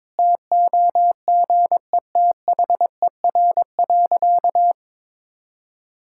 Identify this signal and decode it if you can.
Morse code, heard as a keyed tone, about 22 wpm, 700 Hz. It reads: TOGETHER.
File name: practice / 22wpm / Sentences with 2-Word Phrases - 22wpm